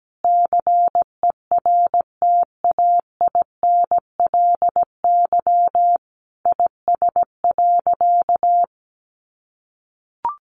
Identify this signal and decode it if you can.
Morse code, heard as a keyed tone, about 17 wpm, 700 Hz. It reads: CERTAINLY IS.
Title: Sentences from Top 900 Words 17wpm